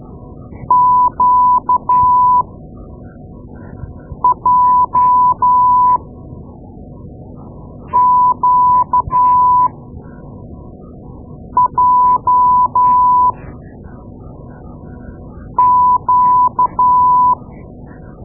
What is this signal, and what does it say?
Morse code, heard as a keyed tone, about 13 wpm, spaced as for 4 wpm, 1000 Hz. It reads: QJQJQ
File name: Morse Lessons / Morse Lessons / Tape-2A